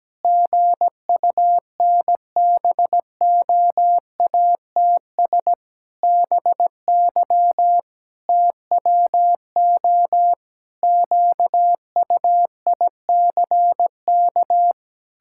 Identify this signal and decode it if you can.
Morse code, heard as a keyed tone, about 17 wpm, 700 Hz. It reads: GUNBOATS BY TWO QUICK
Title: Pangram Challenge v2 - 17wpm